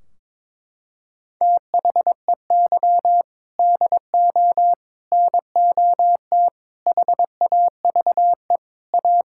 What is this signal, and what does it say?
Morse code, heard as a keyed tone, about 22 wpm, 700 Hz. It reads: THEY DO NOT HAVE A
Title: Sentences from Top 1600 Words 22wpm